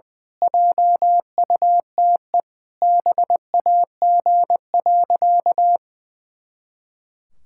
Morse code, heard as a keyed tone, about 21 wpm, 700 Hz. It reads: JUTE BAG.
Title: Pangram Challenge v2 - 20wpm